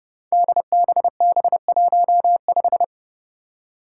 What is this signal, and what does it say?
Morse code, heard as a keyed tone, about 30 wpm, 700 Hz. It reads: DBB15
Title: Five Challenging Letter-Number Combinations 30wpm